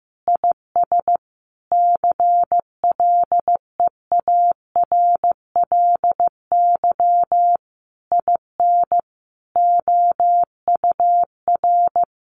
Morse code, heard as a keyed tone, about 15 wpm, 700 Hz. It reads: IS CLEARLY IN OUR
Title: Sentences from Top 1300 Words 15wpm